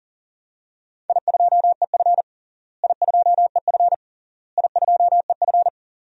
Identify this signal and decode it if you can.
Morse code, heard as a keyed tone, about 40 wpm, 700 Hz. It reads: I2EF I2EF I2EF
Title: DXCC-100 Call Signs - 4 Characters - Repeated 3x - 40wpm